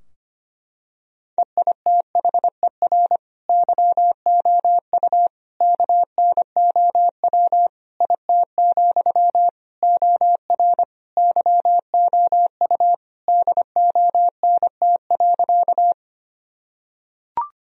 Morse code, heard as a keyed tone, about 25 wpm, 700 Hz. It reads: EITHER YOU KNOW IT, OR YOU DONT.
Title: Sentences from Top 1000 Words - Review Part 4 - 25wpm